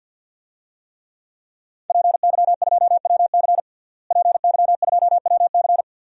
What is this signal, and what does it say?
Morse code, heard as a keyed tone, about 53 wpm, 700 Hz. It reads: PY2WC PY2WC
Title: DXCC-100 Call Signs - 5 Characters - Repeated 2x - 50wpm